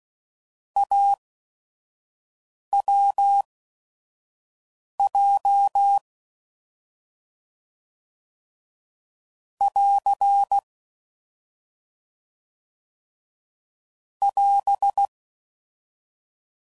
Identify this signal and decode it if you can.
Morse code, heard as a keyed tone, about 16 wpm, spaced as for 5 wpm, 800 Hz. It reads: AWJ <AR> <AS>
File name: Combinations / Difficult Combos / AWJ